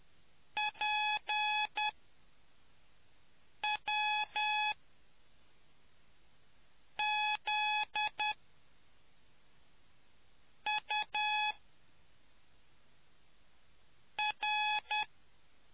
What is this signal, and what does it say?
Morse code, heard as a keyed tone, about 10 wpm, spaced as for 3 wpm, 850 Hz. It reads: PWZUR